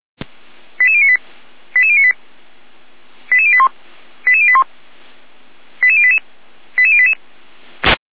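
Five-tone selective call (ZVEI), heard as quick runs of five tones, tones of about 70 ms. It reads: 88098 88098 88081 88081 88088 88088
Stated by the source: ZVEI five-tone selective call received by SDR.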